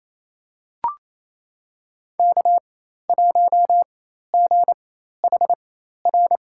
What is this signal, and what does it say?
Morse code, heard as a keyed tone, about 28 wpm, spaced as for 13 wpm, 700 Hz. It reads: K1GHR
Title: USA Call Signs - 5 Characters - 4x Character Spacing - 28wpm